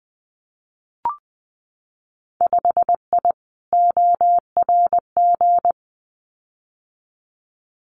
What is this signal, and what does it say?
Morse code, heard as a keyed tone, about 20 wpm, 700 Hz. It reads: 5I ORG
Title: Winter Field Day Exchanges Encore 20wpm